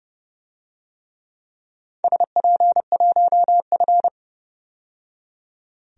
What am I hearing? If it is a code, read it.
Morse code, heard as a keyed tone, about 30 wpm, 700 Hz. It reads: SP1F
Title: DXCC-100 Call Signs - 4 Characters - 30wpm